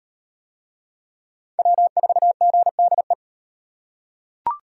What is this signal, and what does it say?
Morse code, heard as a keyed tone, about 38 wpm, 700 Hz. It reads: W4GDE